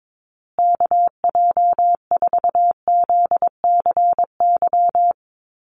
Morse code, heard as a keyed tone, about 22 wpm, 700 Hz. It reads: KJ4ZCY
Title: USA Call Signs Repeated 2x 22wpm